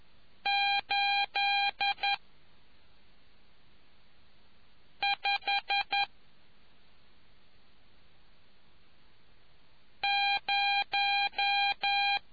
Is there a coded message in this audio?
Yes: Morse code from 0.5 to 12.2 s, heard as a keyed tone, about 11 wpm, 800 Hz. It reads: 8 5 0